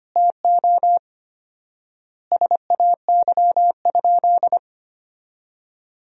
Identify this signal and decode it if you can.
Morse code, heard as a keyed tone, about 25 wpm, 700 Hz. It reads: TO SAY?
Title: 4x Word-Spacing - Sentences from Top 200 Words - 25wpm